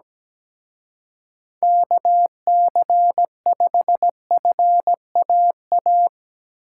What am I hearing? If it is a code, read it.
Morse code, heard as a keyed tone, about 19 wpm, 700 Hz. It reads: KC5FAA
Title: Speed Racing - USA 2x3 Call Signs 17wpm